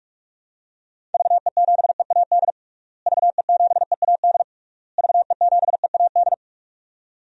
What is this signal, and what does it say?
Morse code, heard as a keyed tone, about 41 wpm, 700 Hz. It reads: VE7EAD VE7EAD VE7EAD
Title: DXCC-100 Call Signs - 6 Characters - Repeated 3x - 45wpm